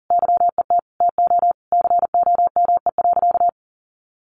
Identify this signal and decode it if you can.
Morse code, heard as a keyed tone, about 40 wpm, 700 Hz. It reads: YET TO COME.
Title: Sentences from Top 1500 Words 40wpm